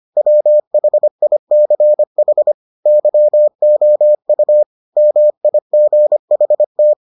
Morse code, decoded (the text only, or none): WHICH YOU MIGHT